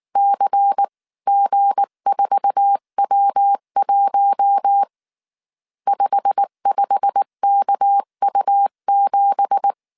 Morse code, heard as a keyed tone, about 19 wpm, 800 Hz. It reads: CG4W1 55KU7
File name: CW / DET / 1 9 1500C S B 18wpm 2551